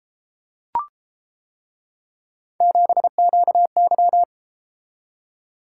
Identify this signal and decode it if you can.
Morse code, heard as a keyed tone, about 31 wpm, 700 Hz. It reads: 7QY